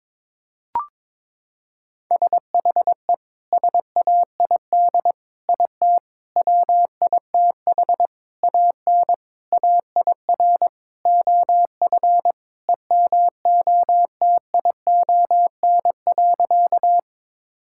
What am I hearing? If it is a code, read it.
Morse code, heard as a keyed tone, about 22 wpm, 700 Hz. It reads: SHE SAID IT WITH AN AIR OF EMOTION.
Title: Sentences from Top 1600 Words 22wpm